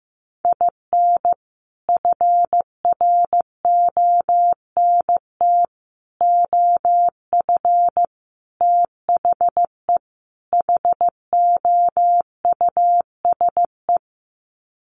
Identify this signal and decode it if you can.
Morse code, heard as a keyed tone, about 15 wpm, 700 Hz. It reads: IN FRONT OF THE HOUSE
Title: Sets of 5 Words Encore2 15wpm